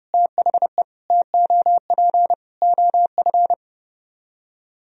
Morse code, heard as a keyed tone, about 35 wpm, 700 Hz. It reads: THE TOP OF
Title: Speed-Racing - Sets of 4 Words 30wpm